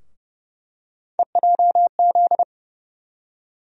Morse code, heard as a keyed tone, about 30 wpm, 700 Hz. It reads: EJZ
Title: USA Call Sign Postfixes - 3 Letters 30wpm